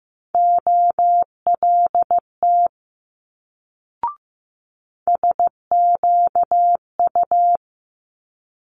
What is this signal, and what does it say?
Morse code, heard as a keyed tone, about 15 wpm, 700 Hz. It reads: OLT SQU
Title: Multi-letter Phonemes Quick Fire 15wpm